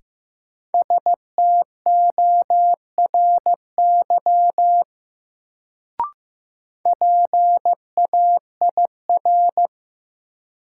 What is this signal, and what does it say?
Morse code, heard as a keyed tone, about 15 wpm, 700 Hz. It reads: STORY PAIR
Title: Sets of 3 Words - New Word Drill - 15wpm